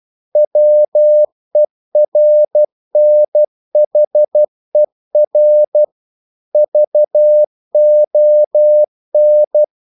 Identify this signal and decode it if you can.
Morse code, heard as a keyed tone, about 12 wpm, 600 Hz. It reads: WERNHER VON
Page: English quotations in CW (telegraphy - morse)